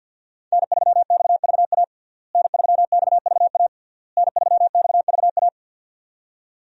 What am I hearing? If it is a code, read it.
Morse code, heard as a keyed tone, about 52 wpm, 700 Hz. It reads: N3XVA N3XVA N3XVA
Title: USA Call Signs - 5 Characters - Repeated 3x - 50wpm